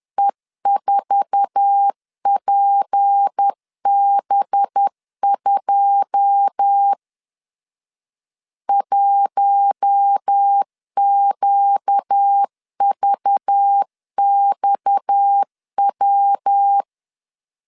Morse code, heard as a keyed tone, about 10 wpm, 800 Hz. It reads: E4PB2 1QVXW
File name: CW / DET / 1 9 1500C S B 10wpm 4439